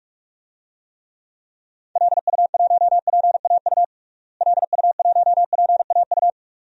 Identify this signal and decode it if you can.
Morse code, heard as a keyed tone, about 47 wpm, 700 Hz. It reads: LU1PAU LU1PAU
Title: DXCC-100 Call Signs - 6 Characters - Repeated 2x - 45wpm